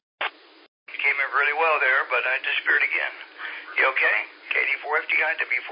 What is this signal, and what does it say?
Emergency Alert System (SAME) tones, heard as harsh buzzing bursts, with speech. It is heard from 0.9 to 4.8 s.